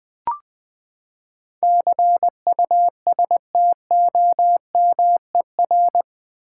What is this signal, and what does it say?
Morse code, heard as a keyed tone, about 20 wpm, 700 Hz. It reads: CUSTOMER
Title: Sentences from Top 1100 Words - New Word Drill v2 - 20wpm